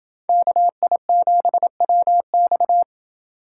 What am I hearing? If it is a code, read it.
Morse code, heard as a keyed tone, about 28 wpm, 700 Hz. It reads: KI7WX